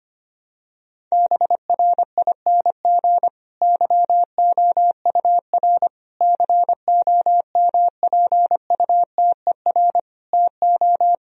Morse code, heard as a keyed tone, about 25 wpm, 700 Hz. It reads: BRING YOUR COMPUTER TO
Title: Sentences from Top 1000 Words - Review Part 3 - 25wpm